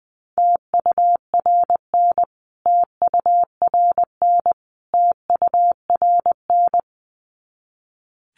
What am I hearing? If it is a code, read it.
Morse code, heard as a keyed tone, about 20 wpm, 700 Hz. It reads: TURN TURN TURN
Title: Top 300 Words Repeated 20wpm